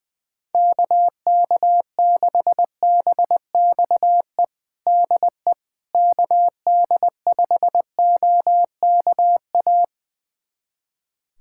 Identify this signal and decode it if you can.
Morse code, heard as a keyed tone, about 21 wpm, 700 Hz. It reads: KK6BXE DE KD5OKA
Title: USA Callsign DE Callsign - 6 Characters - 20wpm